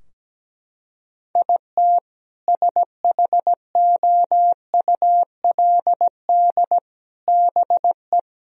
Morse code, heard as a keyed tone, about 17 wpm, 700 Hz. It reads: IT SHOULD BE
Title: Sentences from Top 1000 Words - Review Part 4 - 17wpm